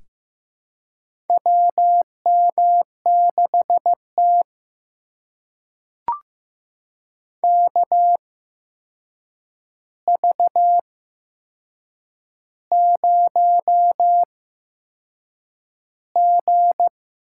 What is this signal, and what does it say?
Morse code, heard as a keyed tone, about 15 wpm, spaced as for 6 wpm, 700 Hz. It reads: WM6T KV0G
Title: USA Call Signs - 4 Characters - 8x Character Spacing - 15wpm